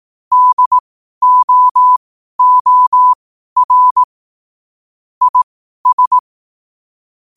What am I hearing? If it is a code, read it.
Morse code, heard as a keyed tone, about 18 wpm, spaced as for 12 wpm, 1000 Hz. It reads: DOOR IS
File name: qotdfiles / 12wpm